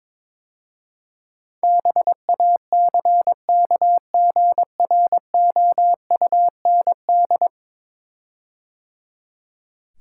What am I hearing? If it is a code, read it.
Morse code, heard as a keyed tone, about 22 wpm, 700 Hz. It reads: BACKGROUND
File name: practice / 22wpm / 10 Letter Words - First 3 Letters Given - 22wpm